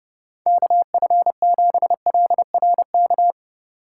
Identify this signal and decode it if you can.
Morse code, heard as a keyed tone, about 36 wpm, 700 Hz. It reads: KF7LRK
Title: Speed Racing - USA 2x3 Call Signs 30wpm